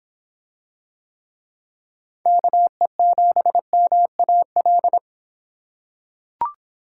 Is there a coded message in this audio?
Yes: Morse code from 2.3 to 5.0 s, heard as a keyed tone, about 26 wpm, 700 Hz. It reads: KE7MAL